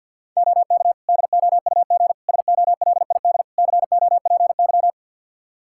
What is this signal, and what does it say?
Morse code, heard as a keyed tone, about 52 wpm, 700 Hz. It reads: OK DOUG SOLID COPY
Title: Practice QSO - Exchange 3 - More Info - 50wpm